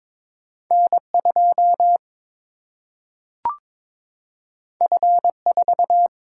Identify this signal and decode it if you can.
Morse code, heard as a keyed tone, about 22 wpm, 700 Hz. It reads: N2 F4
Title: Single-Letter Single-Number 22wpm